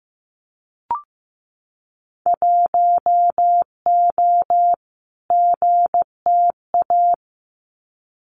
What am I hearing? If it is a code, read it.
Morse code, heard as a keyed tone, about 15 wpm, 700 Hz. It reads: 1O GTA
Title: Winter Field Day Exchanges 15wpm